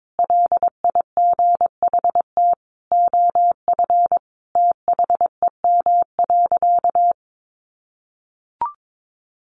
Morse code, heard as a keyed tone, about 22 wpm, 700 Hz. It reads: LIGHT OF THEM.
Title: Sentences from Top 1300 Words 22wpm